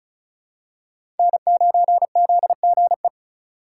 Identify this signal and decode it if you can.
Morse code, heard as a keyed tone, about 36 wpm, 700 Hz. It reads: N9ZGE
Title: USA Call Signs - 5 Characters - 3x Character Spacing - 35wpm